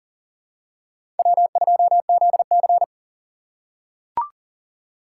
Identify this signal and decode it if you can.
Morse code, heard as a keyed tone, about 40 wpm, 690 Hz. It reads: W2ZC
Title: USA Call Signs - 4 Characters - 2x Character Spacing - 40wpm